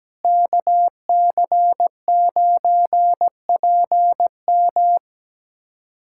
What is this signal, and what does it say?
Morse code, heard as a keyed tone, about 17 wpm, 700 Hz. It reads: KC9PM
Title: USA Callsign DE Callsign - 5 Characters - 17wpm